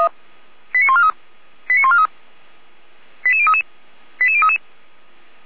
Five-tone selective call (ZVEI), heard as quick runs of five tones, tones of about 70 ms. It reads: 89143 89143 88033 88033